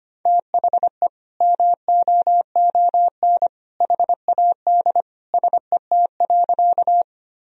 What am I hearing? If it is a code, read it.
Morse code, heard as a keyed tone, about 25 wpm, 700 Hz. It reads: THE MOON HAD SET.
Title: Sentences from Top 1000 Words - Review Part 3 - 25wpm